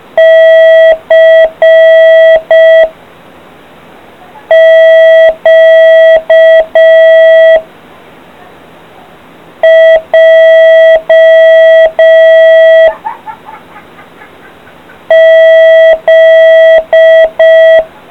Morse code, heard as a keyed tone, about 5 wpm, 650 Hz. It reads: C Q J Z